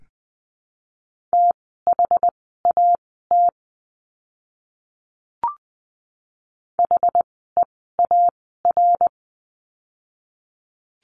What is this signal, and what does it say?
Morse code, heard as a keyed tone, about 20 wpm, spaced as for 14 wpm, 700 Hz. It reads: THAT HEAR
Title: 206 Lesson 33f - Letter j - Words 20wpm